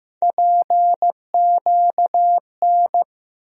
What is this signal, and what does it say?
Morse code, heard as a keyed tone, about 15 wpm, 700 Hz. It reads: PQN
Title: USA Call Sign Postfixes - 3 Letters 15wpm